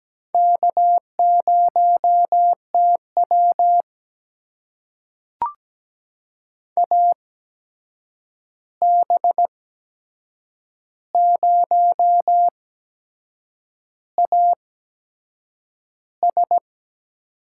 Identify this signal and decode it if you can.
Morse code, heard as a keyed tone, about 17 wpm, spaced as for 7 wpm, 700 Hz. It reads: K0TW AB0AS
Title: USA Call Signs - 8x Character Spacing - 17wpm